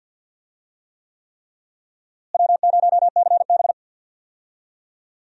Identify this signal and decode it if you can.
Morse code, heard as a keyed tone, about 46 wpm, 700 Hz. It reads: W0CB
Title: USA Call Signs - 4 Characters - Repeated 3x - 50wpm